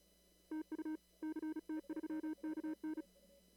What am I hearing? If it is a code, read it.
Morse code, heard as a keyed tone, about 36 wpm, 310 Hz. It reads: TU CT3KN
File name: CW / kt5j wpx cw-op-k5pi